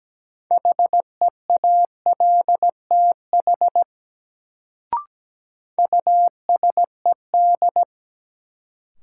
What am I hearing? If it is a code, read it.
Morse code, heard as a keyed tone, about 17 wpm, 700 Hz. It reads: HEALTH USED
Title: Sentences from Top 600 Words - New Word Drill - 17wpm